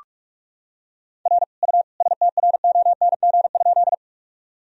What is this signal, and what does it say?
Morse code, heard as a keyed tone, about 42 wpm, 690 Hz. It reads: R U STRONG?
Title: Sentences from Top 500 Words and Abbreviations 45wpm